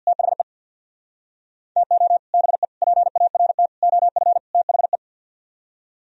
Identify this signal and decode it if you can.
Morse code, heard as a keyed tone, about 64 wpm, 700 Hz. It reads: THE TO BE PART OF THE